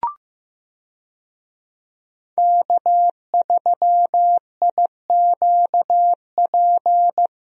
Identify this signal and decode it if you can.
Morse code, heard as a keyed tone, about 15 wpm, 700 Hz. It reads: K3IQP
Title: USA Call Signs - 5 Characters - Repeated 3x - 15wpm